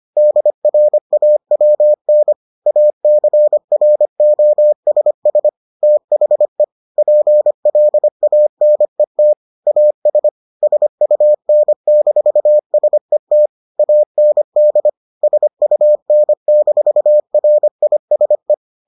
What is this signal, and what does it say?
Morse code, heard as a keyed tone, about 25 wpm, 600 Hz. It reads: DRAWN ACROSS THE PLANET AS SUN-SET AND SUN-RISE